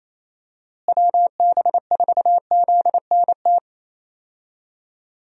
Morse code, heard as a keyed tone, about 28 wpm, 700 Hz. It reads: WB4ZNT
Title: USA Call Signs - 6 Characters - Repeated 2x - 28wpm